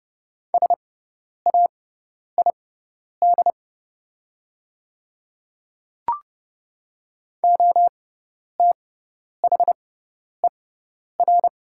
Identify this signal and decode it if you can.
Morse code, heard as a keyed tone, about 30 wpm, spaced as for 10 wpm, 700 Hz. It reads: SAID OTHER
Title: 6x Character Spacing - Top 100 Words - 30wpm